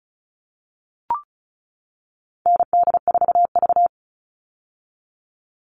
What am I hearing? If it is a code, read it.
Morse code, heard as a keyed tone, about 34 wpm, 700 Hz. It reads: ND4V
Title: All Cut Numbers - 4-Digit 35wpm